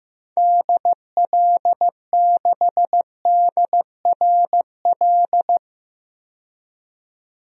Morse code, heard as a keyed tone, about 15 wpm, 700 Hz. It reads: DL6DRL
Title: DXCC-100 Call Signs - 6 Characters - 15wpm